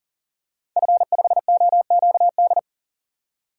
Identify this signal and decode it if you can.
Morse code, heard as a keyed tone, about 40 wpm, 700 Hz. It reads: F5OQD
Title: DXCC-100 Call Signs - 5 Characters - 40wpm